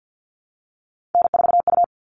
Morse code, heard as a keyed tone, about 50 wpm, 700 Hz. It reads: N4U
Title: All Cut Numbers - 3-Digit 50wpm